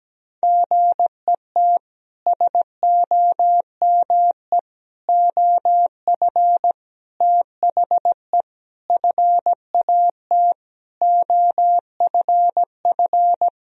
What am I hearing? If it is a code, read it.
Morse code, heard as a keyed tone, about 17 wpm, 700 Hz. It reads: GET SOME OF THE FAT OFF
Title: Sentences from Top 1000 Words - Review Part 4 - 17wpm